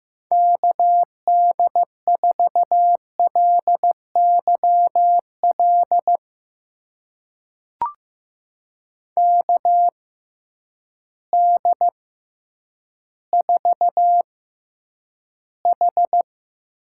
Morse code, heard as a keyed tone, about 15 wpm, spaced as for 9 wpm, 700 Hz. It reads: KD4LYL KD4H
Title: USA Call Signs - 6 Characters - 6x Character Spacing - 15wpm